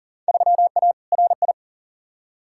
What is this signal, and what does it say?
Morse code, heard as a keyed tone, about 40 wpm, 700 Hz. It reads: 3A RI
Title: Field Day Exchanges - 40wpm